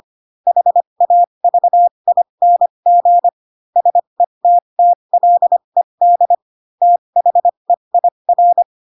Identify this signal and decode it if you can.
Morse code, heard as a keyed tone, about 25 wpm, spaced as for 22 wpm, 700 Hz. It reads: HAVING SETTLED THEIR
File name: Pinocchio / Pinocchio ch2 18 25 ik1hge